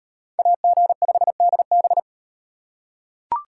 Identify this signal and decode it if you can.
Morse code, heard as a keyed tone, about 38 wpm, 700 Hz. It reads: AG5DB